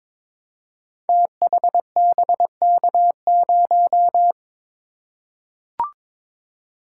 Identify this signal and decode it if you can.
Morse code, heard as a keyed tone, about 22 wpm, 700 Hz. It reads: THBK0
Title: Code Groups - 5 Letter:Numbers - 22wpm